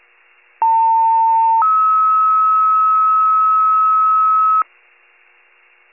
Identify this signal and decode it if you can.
Two-tone page, heard as a one-second tone, then a three-second tone, about 900 then 1290 Hz, with noise. It is heard from 0.6 to 4.6 s.